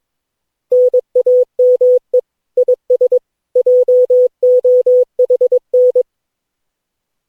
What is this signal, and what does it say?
Morse code, heard as a keyed tone, about 22 wpm, 500 Hz. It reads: NAME IS JOHN
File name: cw / Session9 cw training